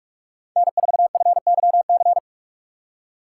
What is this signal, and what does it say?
Morse code, heard as a keyed tone, about 43 wpm, 700 Hz. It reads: N4FYC